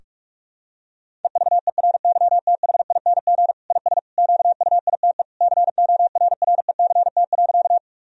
Morse code, heard as a keyed tone, about 47 wpm, 700 Hz. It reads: EVERYTHING IS QUITE CORRECT.
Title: Sentences from Top 1000 Words - Review Part 2 - 45wpm